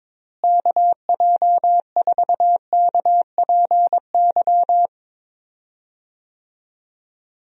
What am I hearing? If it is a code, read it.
Morse code, heard as a keyed tone, about 22 wpm, 700 Hz. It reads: KJ4KPY
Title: USA Call Signs - 6 Characters - Repeated 2x - 22wpm